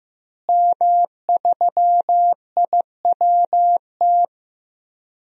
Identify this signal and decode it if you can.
Morse code, heard as a keyed tone, about 15 wpm, 700 Hz. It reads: M3IWT
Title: DXCC-100 Call Signs - 5 Characters - Repeated 3x - 15wpm